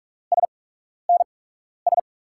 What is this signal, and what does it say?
Morse code, heard as a keyed tone, about 44 wpm, spaced as for 12 wpm, 700 Hz. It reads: SNS
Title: USA Call Signs - 5 Characters - 8x Character Spacing - 45wpm